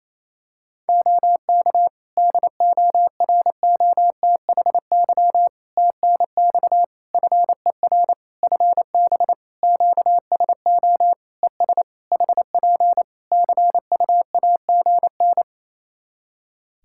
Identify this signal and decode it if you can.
Morse code, heard as a keyed tone, about 28 wpm, 700 Hz. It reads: OK DOROTHY TNX FER FB QSO ES HP CUAGN
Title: Practice QSO - Exchange 4 - Ending - 28wpm